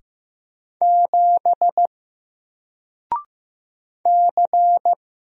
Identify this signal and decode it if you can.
Morse code, heard as a keyed tone, about 15 wpm, 700 Hz. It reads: 7 C
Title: Single Character (Letter:Number:Punctuation:Prosign) Alt 15wpm